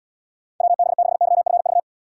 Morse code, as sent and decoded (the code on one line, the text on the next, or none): .-- -... -.... -.-- ..-. ...-
WB6YFV